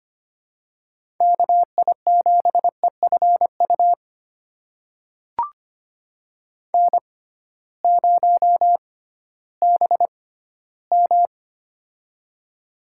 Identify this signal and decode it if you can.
Morse code, heard as a keyed tone, about 25 wpm, spaced as for 14 wpm, 700 Hz. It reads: KI7EFU N0BM
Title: USA Call Signs - 6x Character Spacing - 25wpm